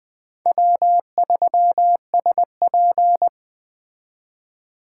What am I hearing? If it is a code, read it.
Morse code, heard as a keyed tone, about 19 wpm, 700 Hz. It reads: W3SP